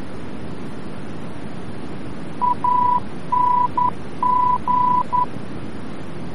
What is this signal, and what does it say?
Morse code, heard as a keyed tone, about 11 wpm, 990 Hz, with noise. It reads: ANG
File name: france / mx / ANG-VOR-F